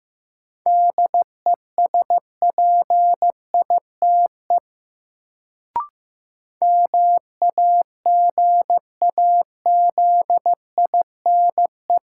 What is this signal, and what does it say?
Morse code, heard as a keyed tone, about 16 wpm, 700 Hz. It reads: DESPITE MAGAZINE
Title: Sentences from Top 1000 Words - Review Part 7 - New Word Drill - 15wpm